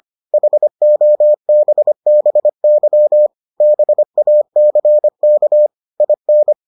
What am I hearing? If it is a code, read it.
Morse code, heard as a keyed tone, about 25 wpm, 600 Hz. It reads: HOBBY BACK IN